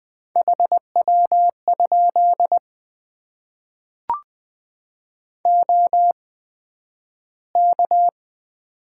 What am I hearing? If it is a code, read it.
Morse code, heard as a keyed tone, about 20 wpm, spaced as for 11 wpm, 700 Hz. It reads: HW? OK
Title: Top 90 QSO Elements - 8x Character Spacing - 20wpm